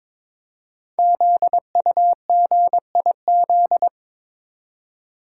Morse code, heard as a keyed tone, about 22 wpm, 700 Hz. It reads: ZUGIZ
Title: Code Groups - 5 Letter:Numbers - 22wpm